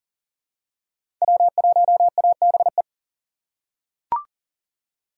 Morse code, heard as a keyed tone, about 40 wpm, 700 Hz. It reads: W1ABE
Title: USA Call Signs - 2x Character Spacing - 40wpm